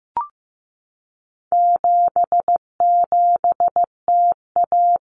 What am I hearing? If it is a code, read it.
Morse code, heard as a keyed tone, about 15 wpm, 700 Hz. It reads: 77TA
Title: Common Cut Numbers - 4-Digit 15wpm